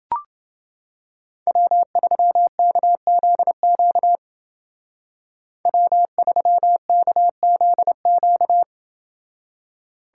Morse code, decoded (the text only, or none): W3KZQ W3KZQ